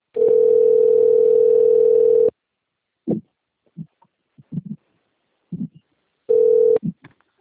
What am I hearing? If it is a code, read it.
Ringback tone (North America), 440 + 480 Hz.